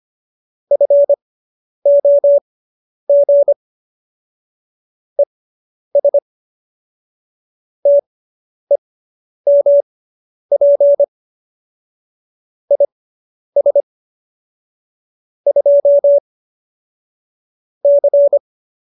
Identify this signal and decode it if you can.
Morse code, heard as a keyed tone, about 25 wpm, spaced as for 10 wpm, 570 Hz. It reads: FOG ES TEMP IS 2 C